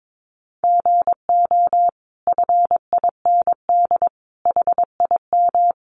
Morse code, heard as a keyed tone, about 22 wpm, 700 Hz. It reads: GO FIND HIM
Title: Sentences from Top 1100 Words 22wpm